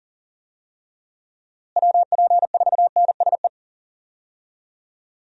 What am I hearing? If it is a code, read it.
Morse code, heard as a keyed tone, about 40 wpm, 700 Hz. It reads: WP4NSE
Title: USA Call Signs - 6 Characters - Repeated 2x - 40wpm